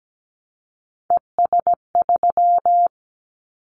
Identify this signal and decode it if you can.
Morse code, heard as a keyed tone, about 17 wpm, 700 Hz. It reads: ES3
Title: Callsign Prefixes for Top 50 DXCC - 17wpm